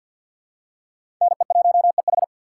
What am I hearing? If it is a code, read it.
Morse code, heard as a keyed tone, about 52 wpm, 700 Hz. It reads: NE1EH
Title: USA Call Signs - 2x Character Spacing - 50wpm